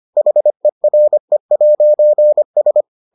Morse code, heard as a keyed tone, about 25 wpm, 600 Hz. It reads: HERE'S